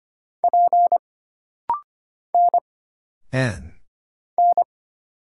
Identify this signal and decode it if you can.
Morse code, heard as a keyed tone, about 25 wpm, 700 Hz. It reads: P N N
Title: Single Letters - Rapid-Fire 25wpm